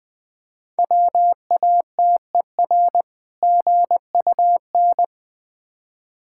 Morse code, heard as a keyed tone, about 21 wpm, 690 Hz. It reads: WATER GUN
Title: Difficult Words with Hints - 20wpm